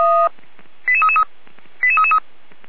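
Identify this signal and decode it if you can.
Five-tone selective call (ZVEI), heard as quick runs of five tones, tones of about 70 ms. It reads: 88333 88333
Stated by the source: ZVEI five-tone call received by SDR.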